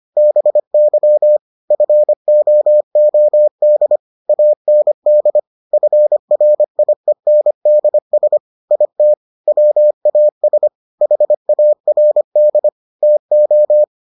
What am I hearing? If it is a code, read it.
Morse code, heard as a keyed tone, about 25 wpm, 600 Hz. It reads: BY FOOD AND FRIENDS IT WAS HARD TO